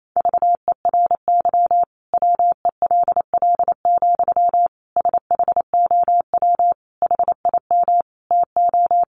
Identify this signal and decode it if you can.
Morse code, heard as a keyed tone, about 28 wpm, 700 Hz. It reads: VERY WELL, SHOW HIM TO
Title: Sentences from Top 200 Words 28wpm